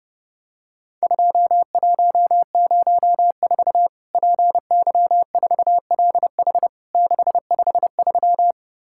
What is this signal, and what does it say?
Morse code, heard as a keyed tone, about 30 wpm, 700 Hz. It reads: 2104 PY4LH 653
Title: Worked All Europe DX Contest - QTC Exchange - 30wpm